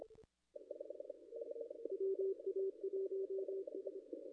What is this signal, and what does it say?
Morse code, heard as a keyed tone, about 26 wpm, 390 Hz. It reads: WA1S
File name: contests / ss-cw-k5tr-op-n5rz -0200